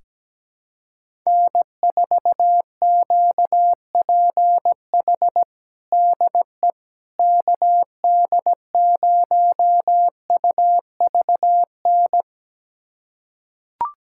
Morse code, heard as a keyed tone, about 17 wpm, 700 Hz. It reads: N4QPH DE KD0UVN
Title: USA Callsign DE Callsign 17wpm